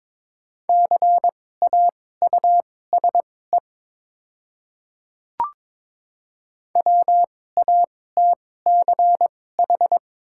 Morse code, heard as a keyed tone, about 22 wpm, spaced as for 16 wpm, 700 Hz. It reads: CAUSE WATCH
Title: 2x Character Spacing - Top 300 Words - 22wpm